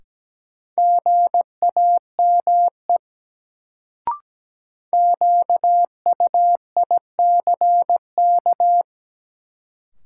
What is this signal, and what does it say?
Morse code, heard as a keyed tone, about 17 wpm, 700 Hz. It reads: GAME QUICK
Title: Sentences from Top 500 Words - New Word Drill - 17wpm